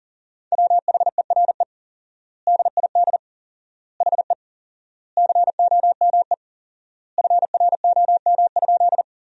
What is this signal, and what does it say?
Morse code, heard as a keyed tone, about 40 wpm, 700 Hz. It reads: WHERE DID HE COME FROM?
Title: 4x Word-Spacing - Sentences from Top 200 Words - 40wpm